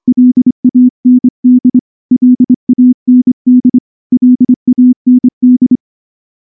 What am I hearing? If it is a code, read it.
Morse code, heard as a keyed tone, about 24 wpm, 250 Hz. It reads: LAND LAND LAND
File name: morse-training-25WPM